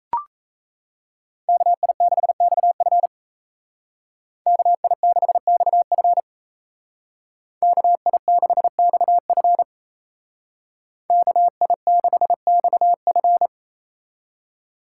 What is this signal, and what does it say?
Morse code, heard as a keyed tone, about 36 wpm, 700 Hz. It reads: KI6XF KI6XF KI6XF KI6XF